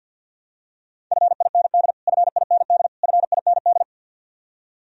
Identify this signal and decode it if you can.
Morse code, heard as a keyed tone, about 53 wpm, 700 Hz. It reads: FIND FIND FIND
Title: Top 100 Words Repeated 50wpm